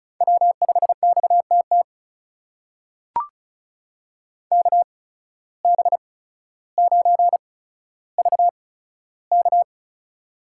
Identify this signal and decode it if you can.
Morse code, heard as a keyed tone, about 36 wpm, spaced as for 15 wpm, 700 Hz. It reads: W5XTT KB9VK
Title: USA Call Signs - 5 Characters - 8x Character Spacing - 35wpm